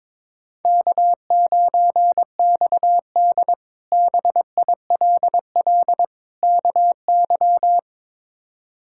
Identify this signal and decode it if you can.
Morse code, heard as a keyed tone, about 22 wpm, 700 Hz. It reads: K9XD BILL KY
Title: K1USN SST QSO Practice - 22wpm